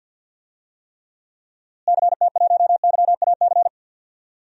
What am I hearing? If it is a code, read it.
Morse code, heard as a keyed tone, about 51 wpm, 700 Hz. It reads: CT1YAC
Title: DXCC-100 Call Signs - 6 Characters - Repeated 2x - 50wpm